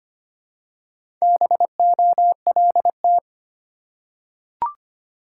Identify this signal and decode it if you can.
Morse code, heard as a keyed tone, about 25 wpm, 690 Hz. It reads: BOLT
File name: practice / 25wpm / Easy Words with Hints - 25wpm